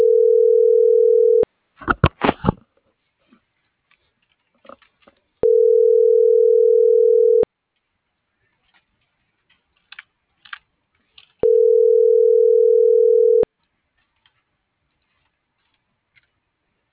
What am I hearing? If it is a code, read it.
Ringback tone (North America), 440 + 480 Hz.
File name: CENTRAL SOTOVA / out alejandro.rodriguez - .49916